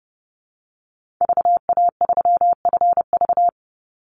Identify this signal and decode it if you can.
Morse code, heard as a keyed tone, about 32 wpm, 700 Hz. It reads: VA3FV
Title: Call Signs - 30wpm